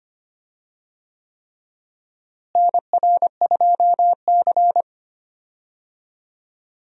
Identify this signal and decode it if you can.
Morse code, heard as a keyed tone, about 25 wpm, 700 Hz. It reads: NR2C
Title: USA Call Signs - 4 Characters - Repeated 3x - 25wpm